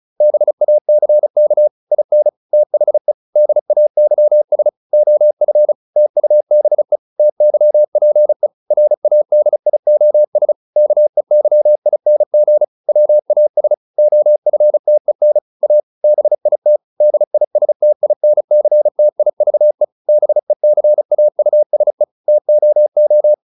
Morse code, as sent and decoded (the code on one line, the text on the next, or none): -... .- -.-. -.- / .. -. / - .... / -.. .- -.-- ... / --- ..-. / - ..- -... / - -.-- .--. / .-. .- -.. .. --- ... / -.- . -.-- .. -. --. / .-- .- ... / --- ..-. - . -. / .- / -... .. - / -.. .. ... - .. -. -.-. - .. ...- . / -... -.-. .- ..- ... / - --- ---
BACK IN THE DAYS OF TUBE TYPE RADIOS KEYING WAS OFTEN A BIT DISTINCTIVE BECAUSE TOO